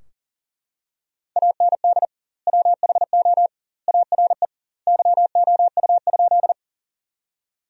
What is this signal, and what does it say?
Morse code, heard as a keyed tone, about 40 wpm, 700 Hz. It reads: AND WHO ARE YOU?
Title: 2x Word-Spacing - Sentences from Top 100 Words - 40wpm